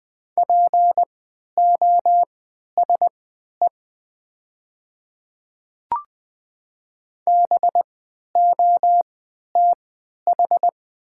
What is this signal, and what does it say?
Morse code, heard as a keyed tone, about 20 wpm, spaced as for 11 wpm, 700 Hz. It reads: POSE BOTH
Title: 3x Character Spacing - Top 500 Words - 20wpm